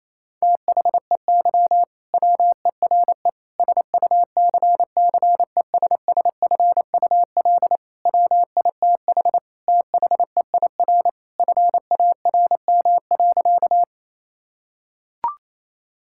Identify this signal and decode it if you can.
Morse code, heard as a keyed tone, about 28 wpm, 700 Hz. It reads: THEY WERE SUCCESSFUL WITH THEIR FARM.
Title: Sentences from Top 1000 Words - Review Part 6 - 28wpm